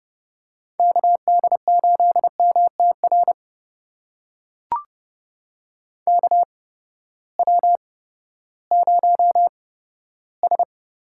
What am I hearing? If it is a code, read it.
Morse code, heard as a keyed tone, about 30 wpm, spaced as for 15 wpm, 700 Hz. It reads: KD8MTR KW0S
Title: USA Call Signs - 8x Character Spacing - 30wpm